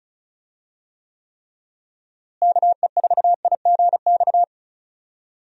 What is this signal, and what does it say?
Morse code, heard as a keyed tone, about 36 wpm, 700 Hz. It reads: KE4IGX
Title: USA Call Signs - 6 Characters - Repeated 3x - 35wpm